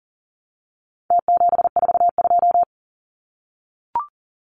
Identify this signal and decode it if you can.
Morse code, heard as a keyed tone, about 40 wpm, 700 Hz. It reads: T742
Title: Numbers - Contest Style - 4-Digit 40wpm